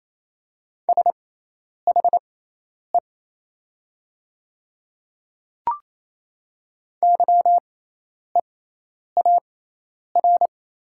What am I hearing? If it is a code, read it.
Morse code, heard as a keyed tone, about 28 wpm, spaced as for 10 wpm, 700 Hz. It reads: SHE YEAR
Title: 6x Character Spacing - Top 200 Words - 28wpm